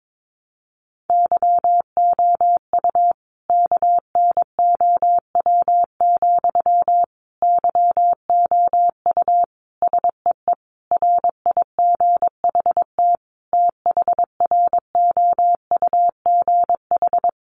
Morse code, heard as a keyed tone, about 22 wpm, 700 Hz. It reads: YOU KNOW, YOU SEE RIGHT THROUGH
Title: Sentences from Top 200 Words 22wpm